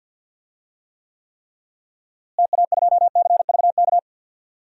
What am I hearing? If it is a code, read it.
Morse code, heard as a keyed tone, about 52 wpm, 700 Hz. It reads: TA2CVK
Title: DXCC-100 Call Signs - 6 Characters - Repeated 3x - 50wpm